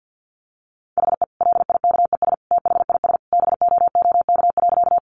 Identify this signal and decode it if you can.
Morse code, heard as a keyed tone, about 50 wpm, 700 Hz. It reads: HE LIKES THIS BOOK.
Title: Additional Easy Phrases 50wpm